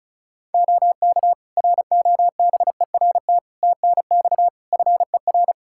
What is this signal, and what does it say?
Morse code, heard as a keyed tone, about 34 wpm, 700 Hz. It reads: OK ROBERT TNX FER
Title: Practice QSO - Exchange 4 - Ending - 35wpm